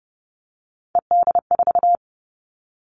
Morse code, heard as a keyed tone, about 30 wpm, 700 Hz. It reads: ED4
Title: All Cut Numbers - 3-Digit 30wpm